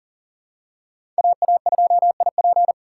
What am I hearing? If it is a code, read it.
Morse code, heard as a keyed tone, about 40 wpm, 700 Hz. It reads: AA2IP
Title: USA Call Signs - 5 Characters - 2x Character Spacing - 40wpm